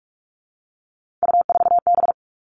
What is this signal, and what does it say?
Morse code, heard as a keyed tone, about 46 wpm, 700 Hz. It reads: U4B
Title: Three Challenging Letter-Number Combinations 45wpm